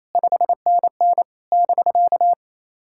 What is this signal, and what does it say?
Morse code, heard as a keyed tone, about 28 wpm, 690 Hz. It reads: 5NN <BK>
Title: SOTA QSO Practice 28wpm